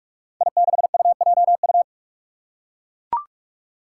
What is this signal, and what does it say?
Morse code, heard as a keyed tone, about 44 wpm, 700 Hz. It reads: I6UJU
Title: Five Challenging Letter-Number Combinations 45wpm